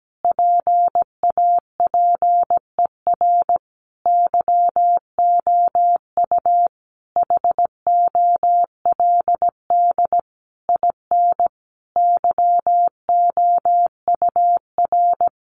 Morse code, heard as a keyed tone, about 17 wpm, 700 Hz. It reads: PAPER YOU HOLD IN YOUR
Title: Sentences from Top 400 Words 17wpm